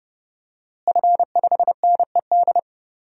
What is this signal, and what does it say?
Morse code, heard as a keyed tone, about 31 wpm, 700 Hz. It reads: F5NED
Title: DXCC-100 Call Signs - 5 Characters - Repeated 3x - 30wpm